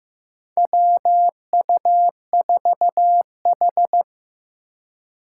Morse code, heard as a keyed tone, about 15 wpm, 700 Hz. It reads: WU4H
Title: USA Call Signs - 3x Character Spacing - 15wpm